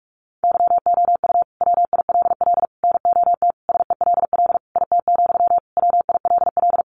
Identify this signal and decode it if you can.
Morse code, heard as a keyed tone, about 44 wpm, 700 Hz. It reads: YOU WILL NOT SELL IT, WILL
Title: Sentences from Top 1100 Words 45wpm